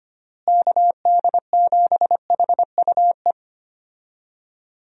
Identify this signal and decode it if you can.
Morse code, heard as a keyed tone, about 25 wpm, 700 Hz. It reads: KD7HUE
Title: USA Call Signs - 6 Characters - Repeated 3x - 25wpm